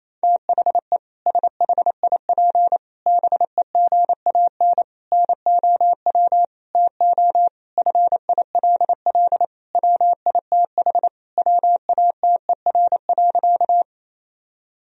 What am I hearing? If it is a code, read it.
Morse code, heard as a keyed tone, about 28 wpm, 700 Hz. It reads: THE SHIP BEGAN NOW TO FILL WITH WATER.
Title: Sentences from Top 1000 Words - Review Part 3 - 28wpm